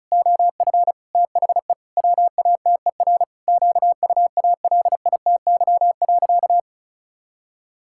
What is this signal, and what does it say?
Morse code, heard as a keyed tone, about 36 wpm, 700 Hz. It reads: OF THE WATER QUALITY.
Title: Sentences from Top 1000 Words - Review Part 5 - 35wpm